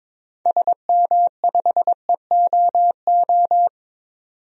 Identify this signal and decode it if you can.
Morse code, heard as a keyed tone, about 22 wpm, 700 Hz. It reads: SM5EOO
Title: DXCC-100 Call Signs - 6 Characters - 22wpm